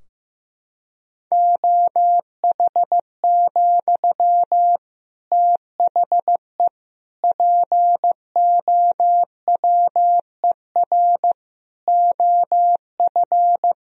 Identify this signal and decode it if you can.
Morse code, heard as a keyed tone, about 15 wpm, 700 Hz. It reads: OH, THE POWER OF
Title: Sentences from Top 1000 Words - Review Part 3 - 15wpm